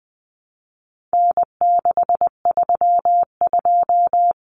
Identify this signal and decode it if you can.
Morse code, heard as a keyed tone, about 20 wpm, 700 Hz. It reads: N632
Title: Common Cut Numbers - 4-Digit 20wpm